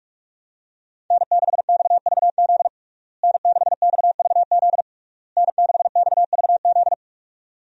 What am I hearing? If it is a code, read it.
Morse code, heard as a keyed tone, about 46 wpm, 700 Hz. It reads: N6XVZ N6XVZ N6XVZ
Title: USA Call Signs - 5 Characters - Repeated 3x - 45wpm